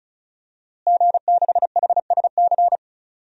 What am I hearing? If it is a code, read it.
Morse code, heard as a keyed tone, about 34 wpm, 700 Hz. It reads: G6HSC
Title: DXCC-100 Call Signs - 5 Characters - Repeated 2x - 35wpm